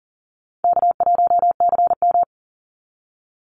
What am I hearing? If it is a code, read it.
Morse code, heard as a keyed tone, about 40 wpm, 700 Hz. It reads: K1CM
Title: NA SOTA Chaser Callsigns 40wpm